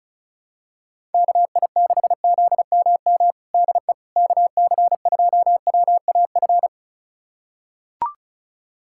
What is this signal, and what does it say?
Morse code, heard as a keyed tone, about 36 wpm, 700 Hz. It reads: KI6ZMM DE KC2WAF
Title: USA Callsign DE Callsign 35wpm